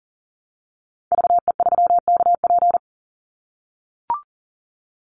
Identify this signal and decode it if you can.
Morse code, heard as a keyed tone, about 40 wpm, 700 Hz. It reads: VE3KP
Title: Call Signs - 40wpm